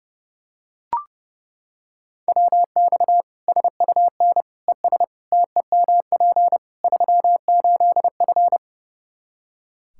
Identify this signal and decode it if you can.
Morse code, heard as a keyed tone, about 30 wpm, 700 Hz. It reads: WX SUN ES TEMP 38F
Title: Practice QSO - Exchange 2 - Info - 30wpm